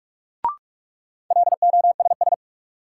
Morse code, heard as a keyed tone, about 41 wpm, 700 Hz. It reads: LOSS
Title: Sentences from Top 1400 Words - New Word Drill - 45wpm